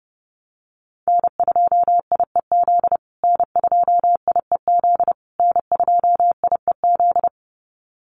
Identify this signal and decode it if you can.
Morse code, heard as a keyed tone, about 30 wpm, 700 Hz. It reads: N2IEZ N2IEZ N2IEZ
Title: USA Call Signs Repeated 3x 30wpm